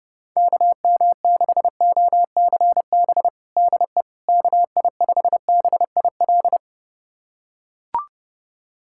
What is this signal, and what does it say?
Morse code, heard as a keyed tone, about 30 wpm, 700 Hz. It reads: KM6OCB DE KI5BIL
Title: USA Callsign DE Callsign - 6 Characters - 30wpm